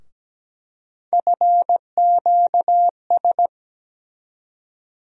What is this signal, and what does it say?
Morse code, heard as a keyed tone, about 17 wpm, 700 Hz. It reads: FQS
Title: USA Call Sign Postfixes - 3 Letters 17wpm